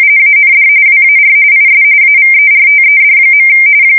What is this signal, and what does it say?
RTTY, about 45.45 baud. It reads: AND WAR IS AVERTED.
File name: fsk170 / battleshipnj